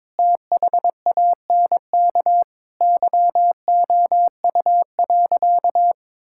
Morse code, heard as a keyed tone, about 21 wpm, 700 Hz. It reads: THANK YOU.
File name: practice / 22wpm / Sentences with 8-Word Phrases - 22wpm